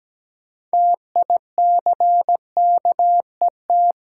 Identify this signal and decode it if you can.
Morse code, heard as a keyed tone, about 17 wpm, 700 Hz. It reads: TICKET
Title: Sentences from Top 1400 Words - New Word Drill - 17wpm